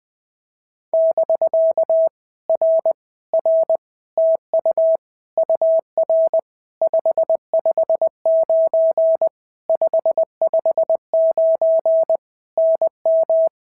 Morse code, heard as a keyed tone, about 20 wpm, 650 Hz. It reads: <BK> R R TU UR 559 559 NM
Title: POTA QSO Practice - 20wpm